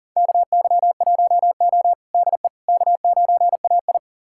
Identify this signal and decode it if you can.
Morse code, heard as a keyed tone, about 37 wpm, 700 Hz. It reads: KY1O DE K9AI
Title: USA Callsign DE Callsign - 4 Characters - 40wpm